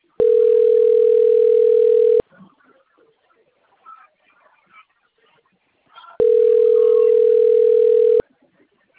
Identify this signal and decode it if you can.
Ringback tone (North America), 440 + 480 Hz, with speech.
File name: rg-1600-unknown- - - .923